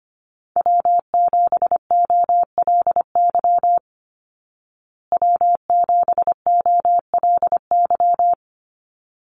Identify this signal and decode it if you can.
Morse code, heard as a keyed tone, about 25 wpm, 700 Hz. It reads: W7OLY W7OLY
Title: Call Signs Repeated 3x - 25wpm